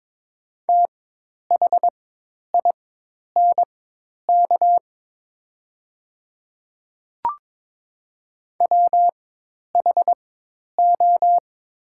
Morse code, heard as a keyed tone, about 22 wpm, spaced as for 10 wpm, 690 Hz. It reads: THINK WHO
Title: 4x Character Spacing - Top 200 Words - 22wpm